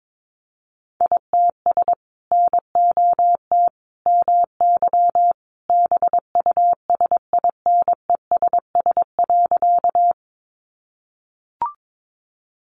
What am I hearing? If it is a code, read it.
Morse code, heard as a keyed tone, about 22 wpm, 700 Hz. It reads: ITS NOT MY BUSINESS.
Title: Sentences from Top 600 Words 22wpm